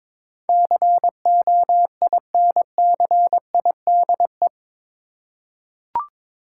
Morse code, heard as a keyed tone, about 22 wpm, 700 Hz. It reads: COINCIDE
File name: practice / 22wpm / Eight Letter Words - 22wpm